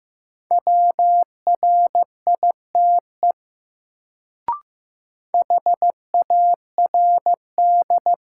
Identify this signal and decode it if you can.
Morse code, heard as a keyed tone, about 15 wpm, 700 Hz. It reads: WRITE HARD
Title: Sentences from Top 300 Words - New Word Drill - 15wpm